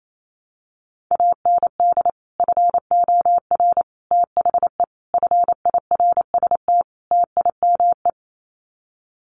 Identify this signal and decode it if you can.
Morse code, heard as a keyed tone, about 28 wpm, 700 Hz. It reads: AND FOR THE FIRST TIME
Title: Sets of 5 Words 28wpm